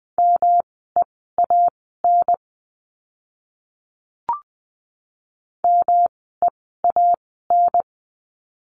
Morse code, heard as a keyed tone, about 20 wpm, spaced as for 14 wpm, 700 Hz. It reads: MEAN MEAN
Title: 108 Lesson 18f - Letter m - Words 20wpm